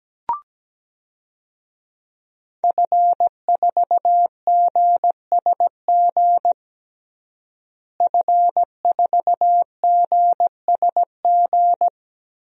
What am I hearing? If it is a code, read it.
Morse code, heard as a keyed tone, about 17 wpm, 700 Hz. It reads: F4GSG F4GSG
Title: DXCC-100 Call Signs - 5 Characters - Repeated 3x - 17wpm